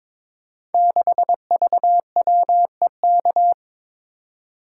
Morse code, heard as a keyed tone, about 22 wpm, 700 Hz. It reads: 6VWEK
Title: Code Groups - 5 Letter:Numbers - 22wpm